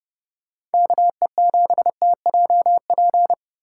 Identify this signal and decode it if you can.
Morse code, heard as a keyed tone, about 30 wpm, 700 Hz. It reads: KE7TJP